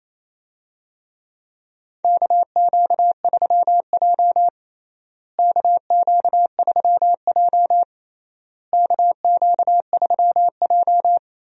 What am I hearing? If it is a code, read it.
Morse code, heard as a keyed tone, about 27 wpm, 700 Hz. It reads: KQ3J KQ3J KQ3J
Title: USA Call Signs - 4 Characters - Repeated 3x - 28wpm